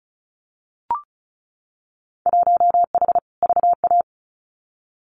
Morse code, heard as a keyed tone, about 34 wpm, 700 Hz. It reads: 1H VA
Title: Winter Field Day Exchanges 35wpm